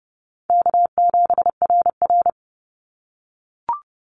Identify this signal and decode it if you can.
Morse code, heard as a keyed tone, about 30 wpm, 700 Hz. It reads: K7RR
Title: Call Signs - 30wpm